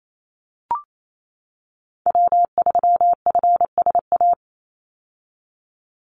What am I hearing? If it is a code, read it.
Morse code, heard as a keyed tone, about 28 wpm, 700 Hz. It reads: W3FSA
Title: Call Signs - 28wpm